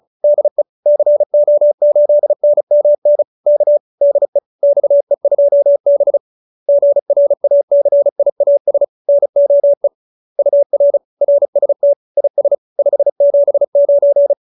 Morse code, heard as a keyed tone, about 34 wpm, 570 Hz. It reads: DE CO8NMN K DE XE2B GRACIAS NOE UR RST IS 579
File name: QSO304 35